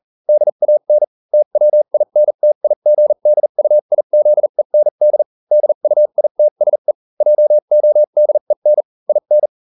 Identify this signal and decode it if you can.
Morse code, heard as a keyed tone, about 40 wpm, spaced as for 35 wpm, 600 Hz. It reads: DAN TWINTIGDUIZEND DUITSE JODEN IN